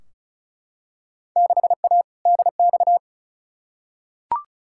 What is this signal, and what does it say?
Morse code, heard as a keyed tone, about 36 wpm, 700 Hz. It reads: 6A DX
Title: Field Day Exchanges Encore - 35wpm